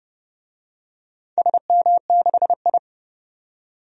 Morse code, heard as a keyed tone, about 31 wpm, 700 Hz. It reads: SM6I
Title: DXCC-100 Call Signs - 4 Characters - 30wpm